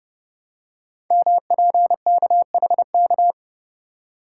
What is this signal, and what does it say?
Morse code, heard as a keyed tone, about 30 wpm, 700 Hz. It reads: MPKHK
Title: Code Groups - 5 Letter:Numbers - 30wpm